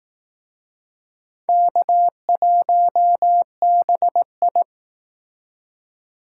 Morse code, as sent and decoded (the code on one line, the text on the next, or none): -.- .---- -... ..
K1BI